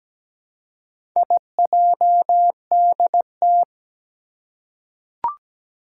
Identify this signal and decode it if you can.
Morse code, heard as a keyed tone, about 17 wpm, 700 Hz. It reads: IJDT
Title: Four Challenging Letter-Number Combinations 17wpm